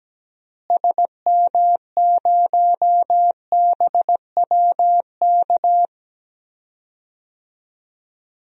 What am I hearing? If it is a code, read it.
Morse code, heard as a keyed tone, about 17 wpm, 700 Hz. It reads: SM0BWK
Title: DXCC-100 Call Signs - 6 Characters - Repeated 3x - 17wpm